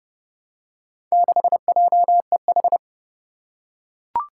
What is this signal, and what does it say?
Morse code, heard as a keyed tone, about 31 wpm, 700 Hz. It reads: 6JEH
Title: Four Challenging Letter-Number Combinations 30wpm